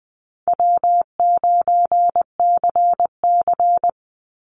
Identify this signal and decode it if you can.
Morse code, heard as a keyed tone, about 20 wpm, 700 Hz. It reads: W9CC
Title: Easy Call Signs Repeated 3x - 20wpm